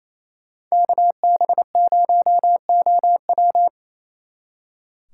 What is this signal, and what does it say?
Morse code, heard as a keyed tone, about 33 wpm, 700 Hz. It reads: KB0OW
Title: Speed Racing - USA 2x2 Call Signs 28wpm